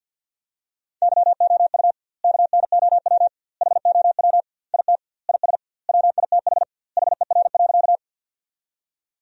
Morse code, heard as a keyed tone, about 50 wpm, 700 Hz. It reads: YOU KNOW HOW IT IS WITH HER.
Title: 2x Word-Spacing - Sentences from Top 100 Words - 50wpm